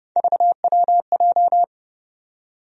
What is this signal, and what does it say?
Morse code, heard as a keyed tone, about 32 wpm, 700 Hz. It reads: VWJ
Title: USA Call Sign Postfixes - 3 Letters 30wpm